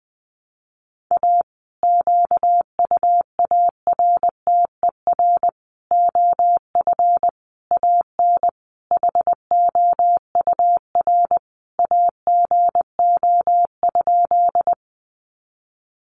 Morse code, heard as a keyed tone, about 20 wpm, 700 Hz. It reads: A QUARTER OF AN HOUR AGO?
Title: Sentences from Top 1300 Words 20wpm